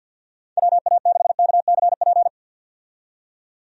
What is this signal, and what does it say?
Morse code, heard as a keyed tone, about 48 wpm, 700 Hz. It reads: WA6KCP
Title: USA Call Signs - 6 Characters - 50wpm